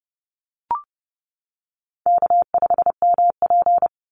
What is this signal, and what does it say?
Morse code, heard as a keyed tone, about 30 wpm, 700 Hz. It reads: K5MP
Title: Easy Call Signs - 30wpm